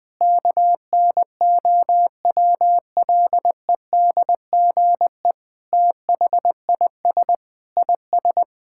Morse code, heard as a keyed tone, about 20 wpm, 700 Hz. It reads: KNOWLEDGE THIS IS
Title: Sentences with 8-Word Phrases - 20wpm